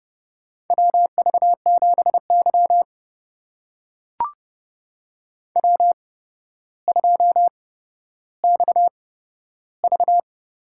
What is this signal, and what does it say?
Morse code, heard as a keyed tone, about 30 wpm, spaced as for 13 wpm, 700 Hz. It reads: WV7Y W2XV
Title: USA Call Signs - 4 Characters - 8x Character Spacing - 30wpm